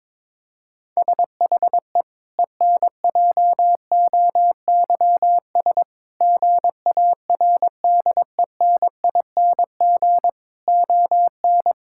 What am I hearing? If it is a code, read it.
Morse code, heard as a keyed tone, about 22 wpm, 700 Hz. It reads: SHE ENJOYS GARDENING ON
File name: Difficult Sentences with Hints - 22wpm